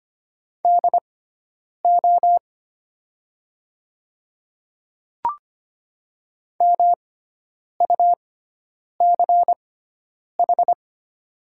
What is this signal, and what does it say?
Morse code, heard as a keyed tone, about 25 wpm, spaced as for 9 wpm, 700 Hz. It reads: DO MUCH
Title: 6x Character Spacing - Top 200 Words - 25wpm